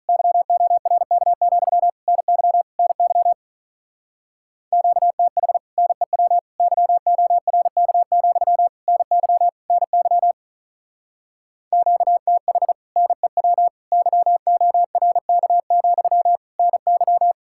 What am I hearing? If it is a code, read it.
Morse code, heard as a keyed tone, about 45 wpm, 700 Hz. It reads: YORK, NY NY QTH NEW YORK, NY NY QTH NEW YORK, NY